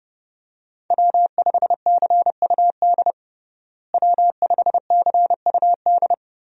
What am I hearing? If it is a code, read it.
Morse code, heard as a keyed tone, about 30 wpm, 700 Hz. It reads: W5CUD W5CUD
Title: USA Call Signs - 5 Characters - Repeated 2x - 30wpm